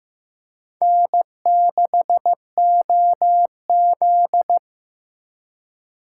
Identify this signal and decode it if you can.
Morse code, heard as a keyed tone, about 18 wpm, 700 Hz. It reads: N6OZ
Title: Speed Racing - USA 1x2 Call Signs 15wpm